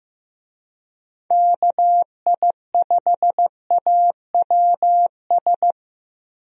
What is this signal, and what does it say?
Morse code, heard as a keyed tone, about 15 wpm, 700 Hz. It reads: KI5AWS
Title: USA Call Signs - 6 Characters - Repeated 2x - 15wpm